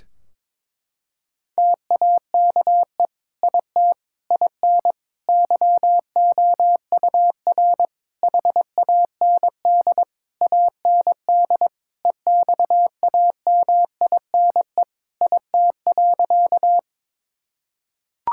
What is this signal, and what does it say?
Morse code, heard as a keyed tone, about 22 wpm, 700 Hz. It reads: TAKE IT IN YOUR HAND AND EXAMINE IT.
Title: Sentences from Top 1600 Words 22wpm